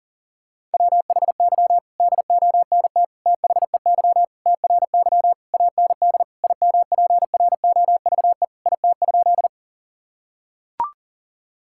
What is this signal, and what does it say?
Morse code, heard as a keyed tone, about 40 wpm, 700 Hz. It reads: WHY DONT THEY TRY AND IMPROVE IT?
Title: Sentences from Top 1000 Words - Review Part 5 - 40wpm